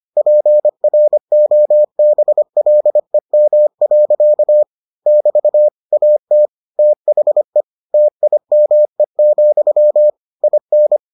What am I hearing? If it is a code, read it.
Morse code, heard as a keyed tone, about 25 wpm, 600 Hz. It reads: PROBLEM. <BT> AT THE TIME, IN